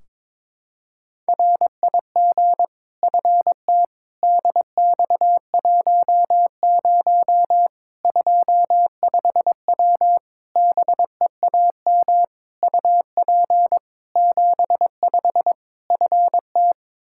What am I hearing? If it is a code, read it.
Morse code, heard as a keyed tone, about 22 wpm, 700 Hz. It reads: RIG FT DX10 25W BEAM UP 75 FT
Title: Practice QSO - Exchange 2 - Info - 22wpm